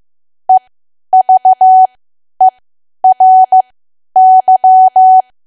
Morse code, heard as a keyed tone, about 15 wpm, spaced as for 10 wpm, 750 Hz. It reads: EVERY
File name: Archive / 10 WPM / 10WPM